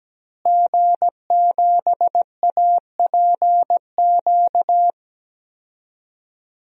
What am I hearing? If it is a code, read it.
Morse code, heard as a keyed tone, about 17 wpm, 700 Hz. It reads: G7APQ
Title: DXCC-100 Call Signs - 5 Characters - Repeated 3x - 17wpm